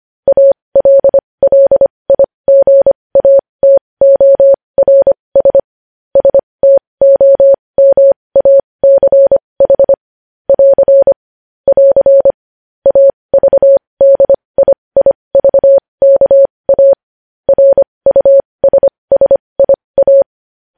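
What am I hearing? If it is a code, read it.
Morse code, heard as a keyed tone, about 25 wpm, spaced as for 20 wpm, 550 Hz. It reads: ALLIGATORS STOMACH <AR> <AR> AVDIIVKA RUSSIA
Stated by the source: Morse code news headlines practice at 20 wpm.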